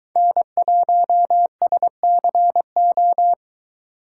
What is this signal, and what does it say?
Morse code, heard as a keyed tone, about 23 wpm, 700 Hz. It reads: N1SCO